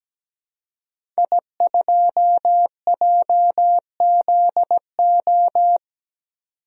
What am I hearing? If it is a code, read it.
Morse code, heard as a keyed tone, about 17 wpm, 700 Hz. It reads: I2JZO
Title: DXCC-100 Call Signs - 5 Characters - 17wpm